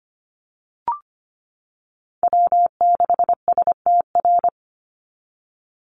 Morse code, heard as a keyed tone, about 25 wpm, 700 Hz. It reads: W6STR
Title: NA SOTA Chaser Callsigns 25wpm